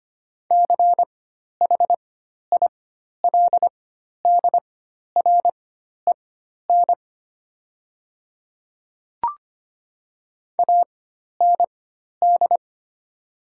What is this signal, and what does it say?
Morse code, heard as a keyed tone, about 25 wpm, spaced as for 12 wpm, 690 Hz. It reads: CHILDREN AND
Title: 4x Character Spacing - Top 500 Words - 25wpm